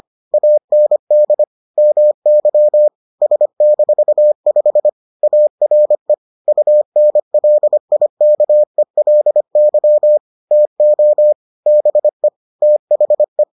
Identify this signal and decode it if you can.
Morse code, heard as a keyed tone, about 25 wpm, 600 Hz. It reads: AND MY S-5 ARE UNLIKELY TO BE THE